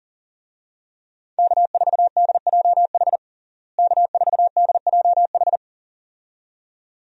Morse code, heard as a keyed tone, about 40 wpm, 700 Hz. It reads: K4DJH K4DJH
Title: USA Call Signs - 5 Characters - Repeated 2x - 40wpm